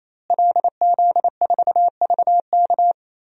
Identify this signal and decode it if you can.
Morse code, heard as a keyed tone, about 28 wpm, 700 Hz. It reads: LZ4VK
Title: DXCC-100 Call Signs - 5 Characters - Repeated 3x - 28wpm